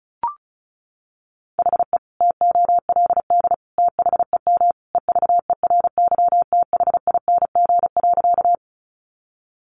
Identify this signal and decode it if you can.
Morse code, heard as a keyed tone, about 34 wpm, 700 Hz. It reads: HE TOLD THEM EVERYTHING.
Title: Sentences from Top 700 Words 35wpm